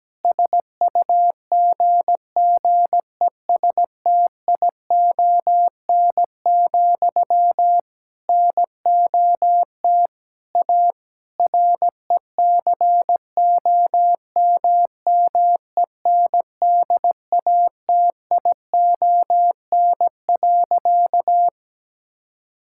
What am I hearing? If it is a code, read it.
Morse code, heard as a keyed tone, about 17 wpm, 700 Hz. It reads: SUGGESTION, NOT A RECOMMENDATION.
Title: Sentences from Top 1600 Words 17wpm